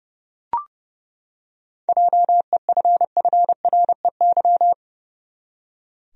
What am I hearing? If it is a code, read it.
Morse code, heard as a keyed tone, about 32 wpm, 700 Hz. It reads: JEFFREY
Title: USA Common Names - 30wpm